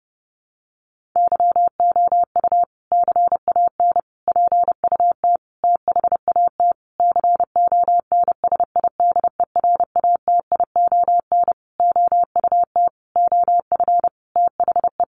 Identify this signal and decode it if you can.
Morse code, heard as a keyed tone, about 30 wpm, 700 Hz. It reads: YOU CAN PUT THAT CONSIDERATION OUT OF THE
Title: Sentences from Top 1500 Words 30wpm